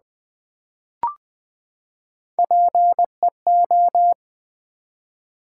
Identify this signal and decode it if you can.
Morse code, heard as a keyed tone, about 20 wpm, 700 Hz. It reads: PEO
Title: Build A Word - 3 Letters At A Time - 6-Letter Words - 20wpm